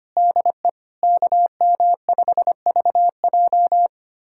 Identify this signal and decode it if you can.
Morse code, heard as a keyed tone, about 25 wpm, 700 Hz. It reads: DE KM5VJ
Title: USA Callsign DE Callsign - 5 Characters - 25wpm